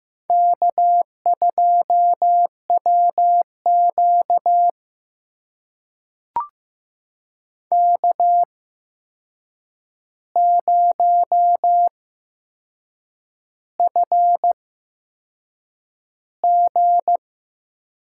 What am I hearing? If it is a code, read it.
Morse code, heard as a keyed tone, about 15 wpm, spaced as for 6 wpm, 700 Hz. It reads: K2WQ K0FG
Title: USA Call Signs - 8x Character Spacing - 15wpm